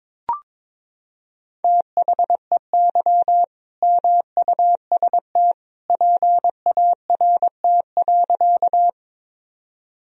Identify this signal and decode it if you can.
Morse code, heard as a keyed tone, about 21 wpm, 700 Hz. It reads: THEY MUST PART.
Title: Sentences from Top 1000 Words - Review Part 1 - 22wpm